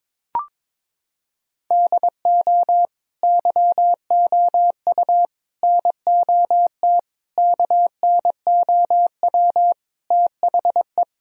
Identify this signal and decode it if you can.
Morse code, heard as a keyed tone, about 23 wpm, 700 Hz. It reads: DO YOU NOT KNOW THE
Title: Sentences from Top 1000 Words - Review Part 3 - 22wpm